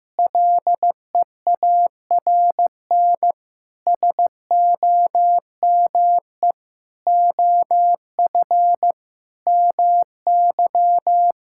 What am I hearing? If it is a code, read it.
Morse code, heard as a keyed tone, about 15 wpm, 700 Hz. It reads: LEARN SOME OF MY
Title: Sentences from Top 1000 Words - Review Part 6 - 15wpm